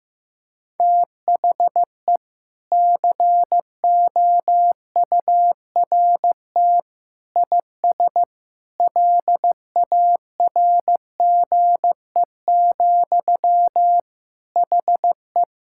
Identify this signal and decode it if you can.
Morse code, heard as a keyed tone, about 15 wpm, 700 Hz. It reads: THE COURT IS LARGE, HE
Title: Sentences from Top 1600 Words 15wpm